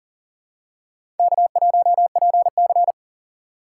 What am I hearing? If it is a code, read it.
Morse code, heard as a keyed tone, about 40 wpm, 700 Hz. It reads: K1PC
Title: USA Call Signs - 4 Characters - 4x Character Spacing - 40wpm